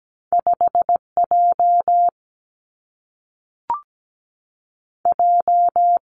Morse code, heard as a keyed tone, about 17 wpm, 700 Hz. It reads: 5 J J
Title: Two Challenging Letter-Number Combinations 17wpm